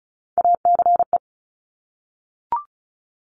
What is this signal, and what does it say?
Morse code, heard as a keyed tone, about 36 wpm, 700 Hz. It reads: ACE
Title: Top Three Letter Combinations in Words 35wpm